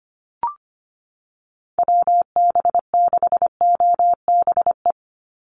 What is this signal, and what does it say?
Morse code, heard as a keyed tone, about 25 wpm, 700 Hz. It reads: WB6OBE
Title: USA Call Signs 25wpm